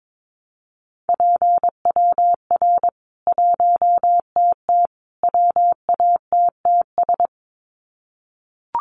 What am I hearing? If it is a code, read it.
Morse code, heard as a keyed tone, about 22 wpm, 700 Hz. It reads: PWR 1TT WATTS
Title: Common QSO Phrases 22wpm